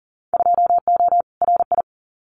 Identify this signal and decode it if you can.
Morse code, heard as a keyed tone, about 40 wpm, 700 Hz. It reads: 2O RI
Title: Winter Field Day Exchanges 40wpm